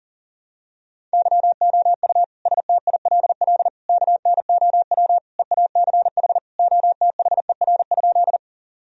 Morse code, heard as a keyed tone, about 40 wpm, 700 Hz. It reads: YOU STILL KNOW EACH OTHER?
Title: Sentences from Top 1000 Words - Review Part 1 - 40wpm